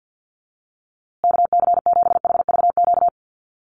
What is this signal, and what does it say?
Morse code, heard as a keyed tone, about 48 wpm, 700 Hz. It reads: KC7HVQ
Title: USA Call Signs Repeated 3x 50wpm